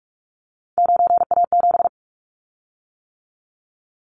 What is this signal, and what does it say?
Morse code, heard as a keyed tone, about 46 wpm, 700 Hz. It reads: 9A7
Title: Callsign Prefixes for Top 100 DXCC - 45wpm